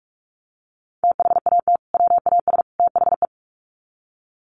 Morse code, heard as a keyed tone, about 44 wpm, 700 Hz. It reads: THAT WAS THE
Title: Sets of 3 Words 45wpm